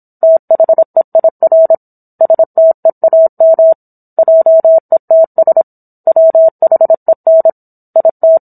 Morse code, heard as a keyed tone, about 27 wpm, 650 Hz. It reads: THEIR STEAM JETS WHEN IT